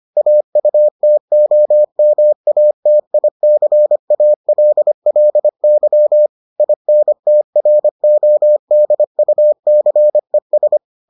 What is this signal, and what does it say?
Morse code, heard as a keyed tone, about 25 wpm, 600 Hz. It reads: AUTOMATICALLY INTRODUCES